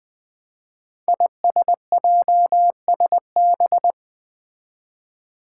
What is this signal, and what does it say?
Morse code, heard as a keyed tone, about 20 wpm, 700 Hz. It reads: ISJSB
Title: Five Challenging Letter-Number Combinations 20wpm